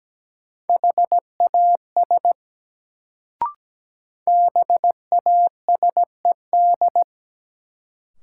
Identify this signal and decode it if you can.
Morse code, heard as a keyed tone, about 17 wpm, 700 Hz. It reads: HAS BASED
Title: Sets of 2 Words - New Word Drill - 17wpm